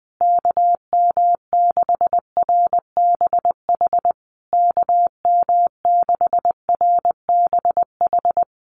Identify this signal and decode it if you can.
Morse code, heard as a keyed tone, about 20 wpm, 700 Hz. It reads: KM6RBH KM6RBH
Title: USA Call Signs Repeated 3x 20wpm